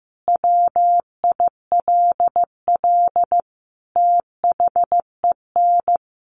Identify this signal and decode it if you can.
Morse code, heard as a keyed tone, about 15 wpm, 700 Hz. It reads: WILL THEN
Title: Sentences from Top 1500 Words 15wpm